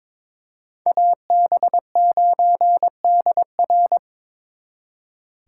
Morse code, heard as a keyed tone, about 27 wpm, 700 Hz. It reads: AB9DR
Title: Speed Racing - USA 2x2 Call Signs 22wpm